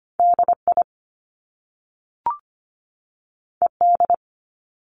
Morse code, heard as a keyed tone, about 25 wpm, 700 Hz. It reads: DI ED
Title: Top Two Letter Combinations in Words 25wpm